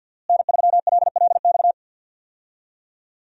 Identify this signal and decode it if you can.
Morse code, heard as a keyed tone, about 52 wpm, 690 Hz. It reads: N3LLX
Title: USA Call Signs - 5 Characters - Repeated 3x - 50wpm